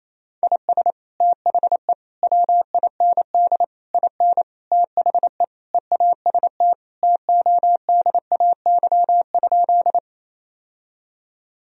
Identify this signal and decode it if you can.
Morse code, heard as a keyed tone, about 28 wpm, 700 Hz. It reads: IS THE WIND IN THE EAST TODAY?
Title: Sentences from Top 1000 Words - Review Part 3 - 28wpm